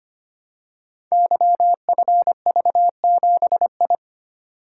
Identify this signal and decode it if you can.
Morse code, heard as a keyed tone, about 25 wpm, 700 Hz. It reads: YFV7I
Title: Code Groups - 5 Letter:Numbers - 25wpm